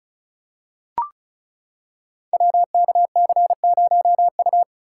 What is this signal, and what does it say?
Morse code, heard as a keyed tone, about 36 wpm, 700 Hz. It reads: WKC0U
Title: Code Groups - 5 Letter:Numbers - 35wpm